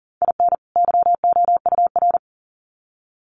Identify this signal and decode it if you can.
Morse code, heard as a keyed tone, about 40 wpm, 700 Hz. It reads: IN YOUR
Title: Sets of 2 Words 40wpm